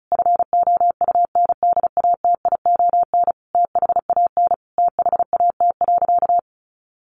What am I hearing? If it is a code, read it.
Morse code, heard as a keyed tone, about 34 wpm, 700 Hz. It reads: FOUNDATION THAN THAT.
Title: Sentences from Top 1200 Words 35wpm